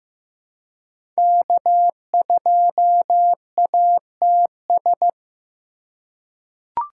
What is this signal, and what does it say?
Morse code, heard as a keyed tone, about 15 wpm, 700 Hz. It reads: K2ATS
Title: USA Call Signs - 5 Characters - 2x Character Spacing - 15wpm